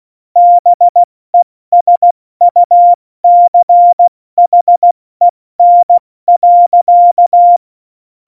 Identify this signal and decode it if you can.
Morse code, heard as a keyed tone, about 16 wpm, spaced as for 14 wpm, 700 Hz. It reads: BESUCHEN.
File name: Morsekurs / Witze / Witz 20. Oktober 2020, 70 F80